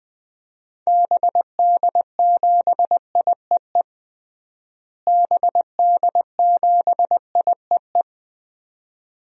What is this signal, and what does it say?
Morse code, heard as a keyed tone, about 20 wpm, 700 Hz. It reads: BD7IEE BD7IEE
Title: DXCC-100 Call Signs - 6 Characters - Repeated 2x - 20wpm